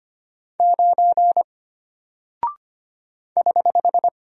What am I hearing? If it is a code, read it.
Morse code, heard as a keyed tone, about 25 wpm, 700 Hz. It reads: 9 <HH>
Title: Single Character (Letter:Number:Punctuation:Prosign) Alt 25wpm